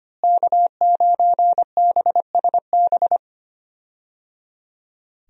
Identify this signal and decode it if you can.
Morse code, heard as a keyed tone, about 25 wpm, 700 Hz. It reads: K9BSB
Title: USA Call Signs - 5 Characters - Repeated 3x - 25wpm